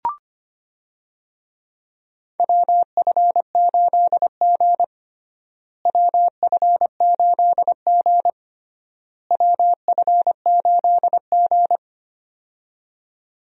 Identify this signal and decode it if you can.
Morse code, heard as a keyed tone, about 25 wpm, 700 Hz. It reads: WF8G WF8G WF8G
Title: USA Call Signs - 4 Characters - Repeated 3x - 25wpm